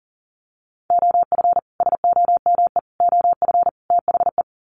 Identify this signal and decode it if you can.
Morse code, heard as a keyed tone, about 40 wpm, 700 Hz. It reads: OF SOME OF THE
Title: Sets of 4 Words 40wpm